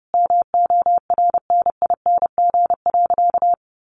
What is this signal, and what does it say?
Morse code, heard as a keyed tone, about 30 wpm, 700 Hz. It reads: MORNING.
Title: Sentences from Top 1300 Words 30wpm